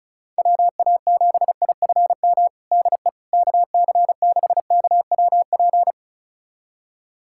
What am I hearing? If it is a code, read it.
Morse code, heard as a keyed tone, about 36 wpm, 700 Hz. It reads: WA7IFM DE KC6KWP
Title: USA Callsign DE Callsign - 6 Characters - 35wpm